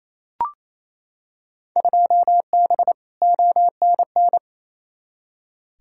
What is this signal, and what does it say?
Morse code, heard as a keyed tone, about 28 wpm, 700 Hz. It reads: 2B ONN
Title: Field Day Exchanges Encore - 28wpm